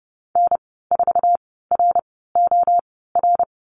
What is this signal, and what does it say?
Morse code, heard as a keyed tone, about 30 wpm, spaced as for 17 wpm, 700 Hz. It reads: N4ROR
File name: Lesson 14g - Number 2 - Call Signs 30wpm